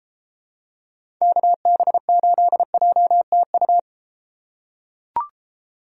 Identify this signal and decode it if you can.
Morse code, heard as a keyed tone, about 32 wpm, 700 Hz. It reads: KB8JTU